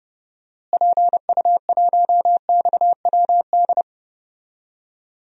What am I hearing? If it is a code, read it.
Morse code, heard as a keyed tone, about 29 wpm, 700 Hz. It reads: PU1XWD
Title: DXCC-100 Call Signs - 6 Characters - Repeated 2x - 30wpm